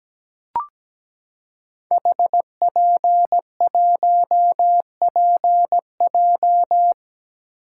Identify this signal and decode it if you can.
Morse code, heard as a keyed tone, about 17 wpm, 700 Hz. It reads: HP1PJ
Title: Code Groups - 5 Letter:Numbers - 17wpm